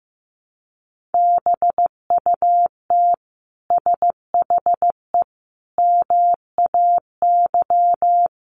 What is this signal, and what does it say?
Morse code, heard as a keyed tone, about 14 wpm, 700 Hz. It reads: BUT SHE MAY
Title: Sentences from Top 600 Words 15wpm